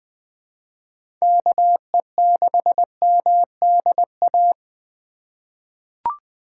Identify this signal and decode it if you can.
Morse code, heard as a keyed tone, about 20 wpm, 700 Hz. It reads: KE6MDA
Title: USA Call Signs - 3x Character Spacing - 20wpm